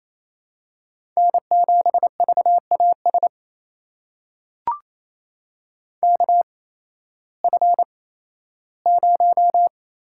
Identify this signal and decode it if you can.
Morse code, heard as a keyed tone, about 28 wpm, spaced as for 15 wpm, 700 Hz. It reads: N7VAS KF0
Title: USA Call Signs - 5 Characters - 8x Character Spacing - 28wpm